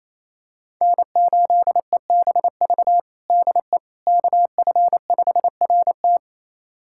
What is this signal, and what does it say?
Morse code, heard as a keyed tone, about 27 wpm, 700 Hz. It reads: N8EBV DE KF5RT
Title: USA Callsign DE Callsign - 5 Characters - 28wpm